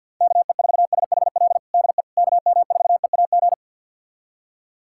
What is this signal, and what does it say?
Morse code, heard as a keyed tone, about 52 wpm, 700 Hz. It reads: KE4SHL DE KM4EAG
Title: USA Callsign DE Callsign - 6 Characters - 50wpm